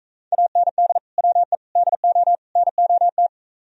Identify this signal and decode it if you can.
Morse code, heard as a keyed tone, about 45 wpm, 700 Hz. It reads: AND WE DO NOT